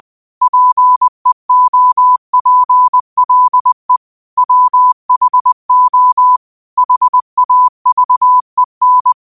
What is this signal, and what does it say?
Morse code, heard as a keyed tone, about 20 wpm, 1000 Hz. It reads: PEOPLE WHO HAVEN